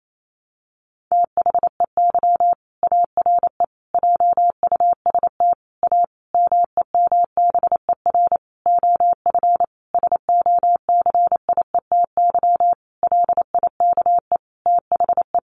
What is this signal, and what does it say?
Morse code, heard as a keyed tone, about 28 wpm, 700 Hz. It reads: THEY ARE JUST A MEMBER OF SOCIETY LIKE THE
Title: Sentences from Top 1100 Words 28wpm